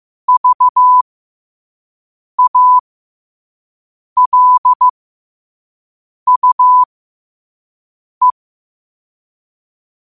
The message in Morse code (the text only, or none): VALUE